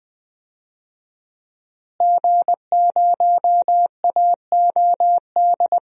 Morse code, heard as a keyed tone, about 20 wpm, 700 Hz. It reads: G0AOD
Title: DXCC-100 Call Signs - 5 Characters - Repeated 2x - 20wpm